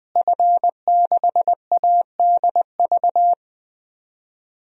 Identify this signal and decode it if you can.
Morse code, heard as a keyed tone, about 20 wpm, 700 Hz. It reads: F6ADV
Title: DXCC-100 Call Signs - 5 Characters - Repeated 3x - 20wpm